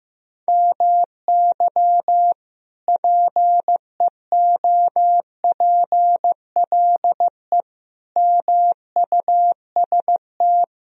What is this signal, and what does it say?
Morse code, heard as a keyed tone, about 15 wpm, 700 Hz. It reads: MY PEOPLE MUST
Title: Sentences from Top 1000 Words - Review Part 3 - 15wpm